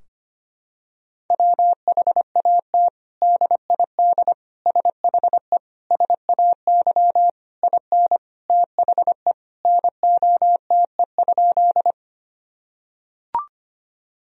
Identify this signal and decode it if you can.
Morse code, heard as a keyed tone, about 25 wpm, 700 Hz. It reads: WHAT DID SHE SAY IN THE NOTE?
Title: Sentences from Top 1000 Words - Review Part 2 - 25wpm